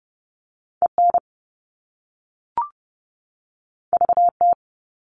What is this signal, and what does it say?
Morse code, heard as a keyed tone, about 30 wpm, 700 Hz. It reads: EN VT
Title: All Cut Numbers - 2-Digit 30wpm